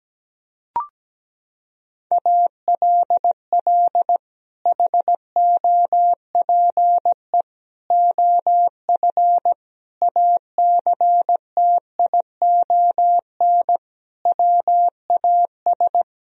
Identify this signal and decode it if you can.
Morse code, heard as a keyed tone, about 17 wpm, 700 Hz. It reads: ALL HOPE OF ACTION WAS
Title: Sentences from Top 1000 Words - Review Part 4 - 17wpm